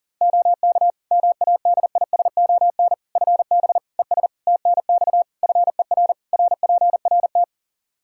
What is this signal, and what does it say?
Morse code, heard as a keyed tone, about 40 wpm, 700 Hz. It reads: OK MADISON FB ES TNX FER RPRT
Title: Practice QSO - Exchange 2 - Info - 40wpm